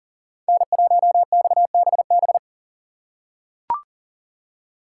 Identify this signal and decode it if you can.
Morse code, heard as a keyed tone, about 40 wpm, 700 Hz. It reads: N1XBB
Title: USA Call Signs - 5 Characters - 3x Character Spacing - 40wpm